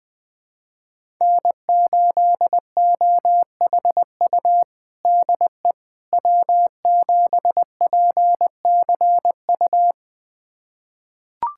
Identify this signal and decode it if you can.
Morse code, heard as a keyed tone, about 20 wpm, 700 Hz. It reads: N8OHU DE W7PCU
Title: USA Callsign DE Callsign - 5 Characters - 20wpm